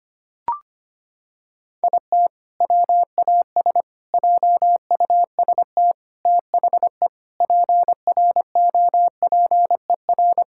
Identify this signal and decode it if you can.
Morse code, heard as a keyed tone, about 25 wpm, 700 Hz. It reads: IT WAS JUST THE PROPER
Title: Sentences from Top 1000 Words - Review Part 5 - 25wpm